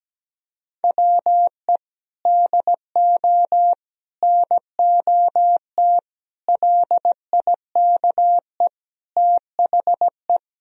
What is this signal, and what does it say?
Morse code, heard as a keyed tone, about 17 wpm, 700 Hz. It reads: WE DO NOT LIKE THE
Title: Sentences from Top 1000 Words - Review Part 4 - 17wpm